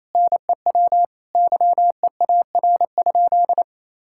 Morse code, heard as a keyed tone, about 28 wpm, 700 Hz. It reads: NEW YEAR?
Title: Sentences from Top 1000 Words - Review Part 5 - 28wpm